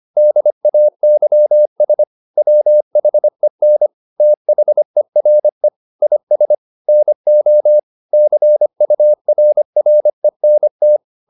Morse code, heard as a keyed tone, about 25 wpm, 600 Hz. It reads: DAYS WHEN THERE IS NO CURRENT